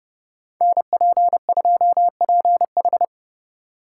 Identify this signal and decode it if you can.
Morse code, heard as a keyed tone, about 30 wpm, 700 Hz. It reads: NP2PH